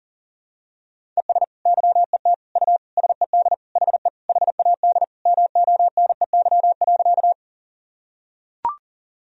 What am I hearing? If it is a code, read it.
Morse code, heard as a keyed tone, about 40 wpm, 700 Hz. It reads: ES YET U SED HE HAD MONEY.
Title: Sentences from Top 500 Words and Abbreviations 40wpm